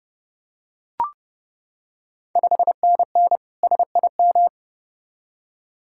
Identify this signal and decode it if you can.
Morse code, heard as a keyed tone, about 31 wpm, 700 Hz. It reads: 5NN SIM
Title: Ontario QSO Party Exchange 30wpm